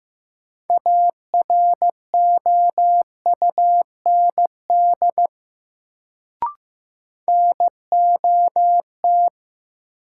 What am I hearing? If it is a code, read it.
Morse code, heard as a keyed tone, about 15 wpm, 700 Hz. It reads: AROUND NOT
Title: Sentences from Top 600 Words - New Word Drill - 15wpm